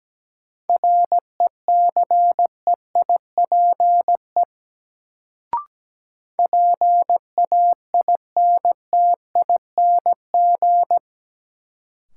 Morse code, heard as a keyed tone, about 17 wpm, 700 Hz. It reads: RECEIPE PAINTING
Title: Sentences from Top 1200 Words - New Word Drill Alt - 17wpm